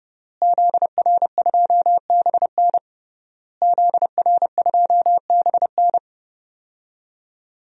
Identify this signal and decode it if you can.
Morse code, heard as a keyed tone, about 32 wpm, 700 Hz. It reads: ZR2BN ZR2BN
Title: DXCC-100 Call Signs - 5 Characters - Repeated 3x - 30wpm